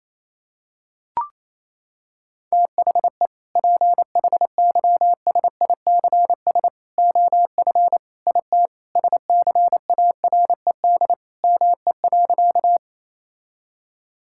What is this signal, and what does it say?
Morse code, heard as a keyed tone, about 28 wpm, 700 Hz. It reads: THE PHYSICS OF IT SCARED ME.
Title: Sentences from Top 1000 Words - Review Part 5 - 28wpm